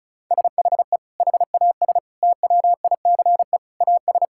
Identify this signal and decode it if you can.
Morse code, heard as a keyed tone, about 36 wpm, 700 Hz. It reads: SHE HAS TWICE AS
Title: Sentences from Top 1600 Words 35wpm